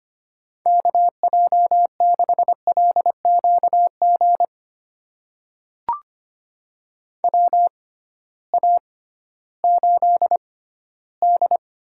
Morse code, heard as a keyed tone, about 25 wpm, spaced as for 14 wpm, 700 Hz. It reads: KJ6LQG WA8D
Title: USA Call Signs - 6 Characters - 6x Character Spacing - 25wpm